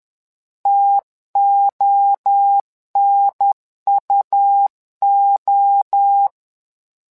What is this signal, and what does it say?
Morse code, heard as a keyed tone, about 10 wpm, 800 Hz. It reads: TONUO
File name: LEZIONI / 3 MWRO / 3 MWRO 10wpm 2148